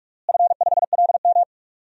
Morse code, heard as a keyed tone, about 41 wpm, 700 Hz. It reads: F5LM
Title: DXCC-100 Call Signs - 4 Characters - Repeated 3x - 45wpm